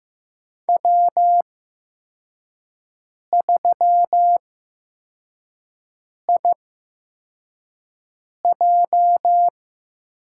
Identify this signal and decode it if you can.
Morse code, heard as a keyed tone, about 15 wpm, spaced as for 4 wpm, 700 Hz. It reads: W3IJ
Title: USA Call Signs - 5 Characters - 8x Character Spacing - 15wpm